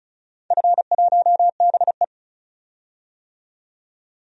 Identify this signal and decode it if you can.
Morse code, heard as a keyed tone, about 34 wpm, 700 Hz. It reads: F1BE
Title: DXCC-100 Call Signs - 4 Characters - 35wpm